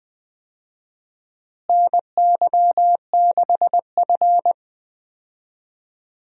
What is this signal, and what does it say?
Morse code, heard as a keyed tone, about 20 wpm, 700 Hz. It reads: NY6F
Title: USA Call Signs - 4 Characters - Repeated 3x - 20wpm